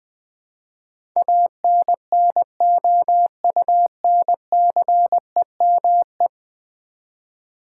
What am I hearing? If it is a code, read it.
Morse code, heard as a keyed tone, about 19 wpm, 700 Hz. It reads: ANNOUNCEME
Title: Build A Word - 5 Letters At A Time - 12-Letter Words - 20wpm